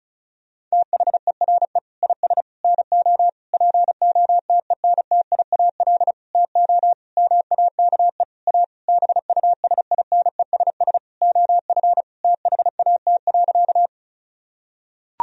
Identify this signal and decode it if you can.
Morse code, heard as a keyed tone, about 36 wpm, 700 Hz. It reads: THERE IS NO POTENTIAL TO MAKE A BUSINESS OF THAT.
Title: Sentences from Top 1000 Words - Review Part 4 - 35wpm